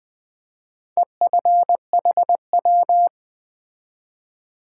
Morse code, heard as a keyed tone, about 20 wpm, 700 Hz. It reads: EFHW
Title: Top 180 QSO Elements 20wpm